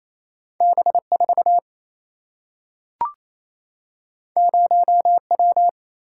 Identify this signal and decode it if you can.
Morse code, heard as a keyed tone, about 28 wpm, 700 Hz. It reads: B4 0W
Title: Single-Letter Single-Number 28wpm